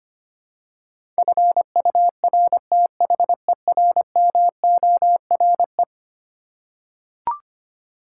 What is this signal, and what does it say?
Morse code, heard as a keyed tone, about 25 wpm, 700 Hz. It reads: FURTHERMORE
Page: morse-code-ninja-practice-25wpm